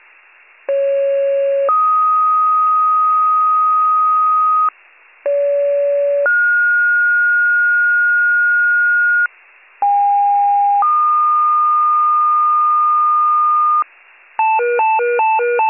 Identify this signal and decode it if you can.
Two-tone page, heard as a one-second tone, then a three-second tone, about 550 then 1250 Hz, about 570 then 1430 Hz, about 810 then 1220 Hz. It is heard from 0.7 to 13.8 s.